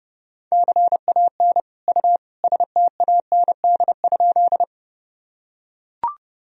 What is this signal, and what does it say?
Morse code, heard as a keyed tone, about 32 wpm, 700 Hz. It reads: CAN U STAND?
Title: Sentences from Top 300 Words and Abbreviations 30wpm